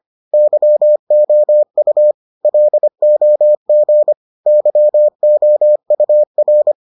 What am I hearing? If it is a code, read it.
Morse code, heard as a keyed tone, about 25 wpm, 600 Hz. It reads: YOU LOG YOUR